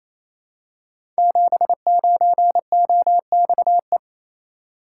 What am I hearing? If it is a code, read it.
Morse code, heard as a keyed tone, about 28 wpm, 700 Hz. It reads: 79OXE
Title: Code Groups - 5 Letter:Numbers - 28wpm